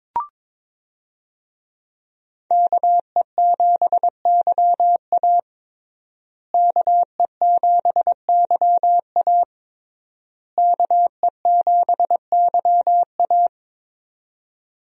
Morse code, heard as a keyed tone, about 22 wpm, 700 Hz. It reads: KE7YA KE7YA KE7YA
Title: USA Call Signs - 5 Characters - Repeated 3x - 22wpm